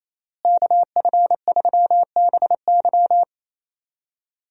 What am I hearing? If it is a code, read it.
Morse code, heard as a keyed tone, about 28 wpm, 700 Hz. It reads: KF3BY
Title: USA Call Signs - 5 Characters - Repeated 3x - 28wpm